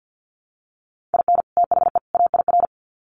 Morse code, heard as a keyed tone, about 48 wpm, 700 Hz. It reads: IN THE AIR
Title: Sets of 3 Words 50wpm